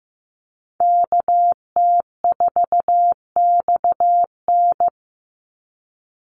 Morse code, heard as a keyed tone, about 15 wpm, 700 Hz. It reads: KT4XN
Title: Call Signs Repeated 3x - 15wpm